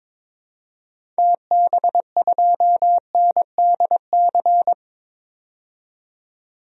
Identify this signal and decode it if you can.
Morse code, heard as a keyed tone, about 22 wpm, 700 Hz. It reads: TB2NDC
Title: DXCC-100 Call Signs - 6 Characters - 22wpm